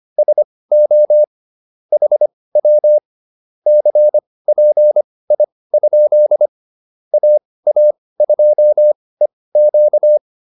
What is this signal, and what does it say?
Morse code, heard as a keyed tone, about 25 wpm, spaced as for 18 wpm, 600 Hz. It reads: SO HW CPI? AA2EQ